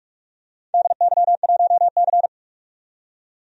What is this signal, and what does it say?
Morse code, heard as a keyed tone, about 46 wpm, 700 Hz. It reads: DY1C
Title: DXCC-100 Call Signs - 4 Characters - 45wpm